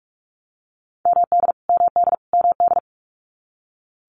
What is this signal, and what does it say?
Morse code, heard as a keyed tone, about 44 wpm, 700 Hz. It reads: MD MD MD
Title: US States Repeated 3x 45wpm